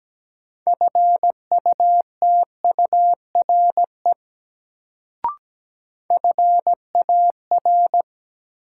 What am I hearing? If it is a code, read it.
Morse code, heard as a keyed tone, about 17 wpm, 700 Hz. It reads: FUTURE FAR
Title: Sets of 3 Words - New Word Drill - 17wpm